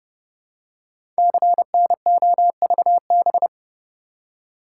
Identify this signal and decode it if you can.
Morse code, heard as a keyed tone, about 29 wpm, 700 Hz. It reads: CNOVB
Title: Code Groups - 5 Letters - 30wpm